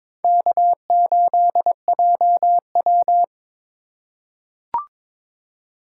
Morse code, heard as a keyed tone, about 22 wpm, 700 Hz. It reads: K8JW
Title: USA Call Signs - 3x Character Spacing - 22wpm